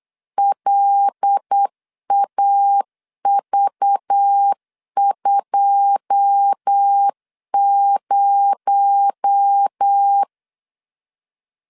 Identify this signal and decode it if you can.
Morse code, heard as a keyed tone, about 8 wpm, 800 Hz. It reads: LAV20
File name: LEZIONI / 10 SIGNS / 10B INT 8wpm 3534